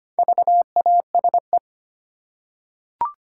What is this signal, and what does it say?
Morse code, heard as a keyed tone, about 25 wpm, 690 Hz. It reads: VASE
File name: practice / 25wpm / Easy Words with Hints - 25wpm